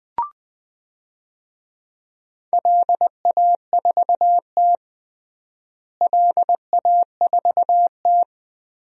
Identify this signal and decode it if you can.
Morse code, heard as a keyed tone, about 20 wpm, 700 Hz. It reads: LA4T LA4T
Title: DXCC-100 Call Signs - 4 Characters - Repeated 3x - 20wpm